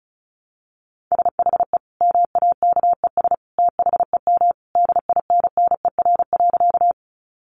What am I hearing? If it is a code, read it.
Morse code, heard as a keyed tone, about 34 wpm, 700 Hz. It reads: SHE MAKES THEM DINNER.
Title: Easy Phrases 35wpm